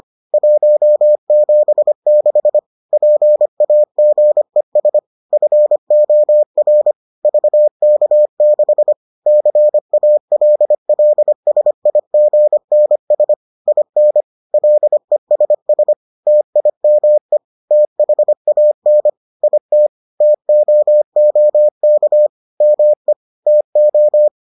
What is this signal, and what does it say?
Morse code, heard as a keyed tone, about 25 wpm, 600 Hz. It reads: 176 PAGES FOR VK6 CALLSIGNS IN LESS TIME THAN IT TOOK ME TO